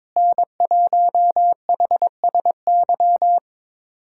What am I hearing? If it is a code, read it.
Morse code, heard as a keyed tone, about 22 wpm, 700 Hz. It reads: N1HSY
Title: USA Callsign DE Callsign - 5 Characters - 22wpm